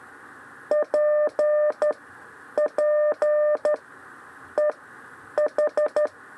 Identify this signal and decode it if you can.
Morse code, heard as a keyed tone, about 13 wpm, 600 Hz. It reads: P P E H
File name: 6 lezione cw 2